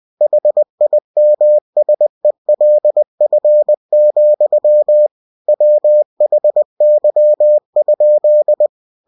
Morse code, heard as a keyed tone, about 20 wpm, 600 Hz. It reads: HIMSELF, WHY?
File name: cw-books / winnie-the-pooh / winnie-20wpm-page-0026